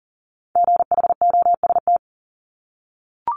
Morse code, heard as a keyed tone, about 38 wpm, 690 Hz. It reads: GHOST
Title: Multi-letter Phonemes 40wpm